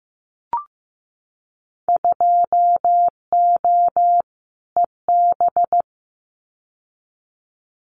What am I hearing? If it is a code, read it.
Morse code, heard as a keyed tone, about 15 wpm, 700 Hz. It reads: 2O EB
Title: Winter Field Day Exchanges Encore 15wpm